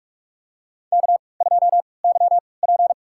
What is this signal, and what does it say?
Morse code, heard as a keyed tone, about 47 wpm, spaced as for 26 wpm, 700 Hz. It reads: K2YP
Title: USA Call Signs - 4 Characters - 3x Character Spacing - 45wpm